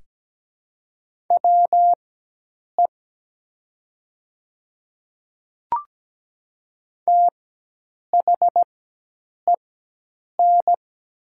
Morse code, heard as a keyed tone, about 17 wpm, spaced as for 8 wpm, 700 Hz. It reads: WE THEN
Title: 4x Character Spacing - Top 300 Words - 17wpm